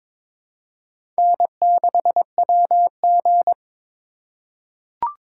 Morse code, heard as a keyed tone, about 22 wpm, 700 Hz. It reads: N6WG
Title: USA Call Signs - 4 Characters - 4x Character Spacing - 22wpm